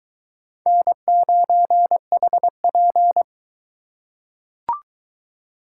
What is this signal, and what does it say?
Morse code, heard as a keyed tone, about 22 wpm, 700 Hz. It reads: N9HP